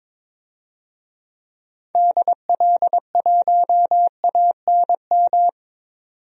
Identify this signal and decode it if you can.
Morse code, heard as a keyed tone, about 22 wpm, 700 Hz. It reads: DL1ANM
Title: DXCC-100 Call Signs - 6 Characters - 22wpm